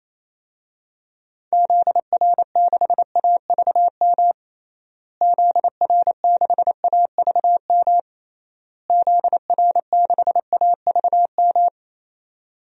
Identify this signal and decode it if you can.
Morse code, heard as a keyed tone, about 28 wpm, 700 Hz. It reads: ZR6AVM ZR6AVM ZR6AVM
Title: DXCC-100 Call Signs - 6 Characters - Repeated 3x - 28wpm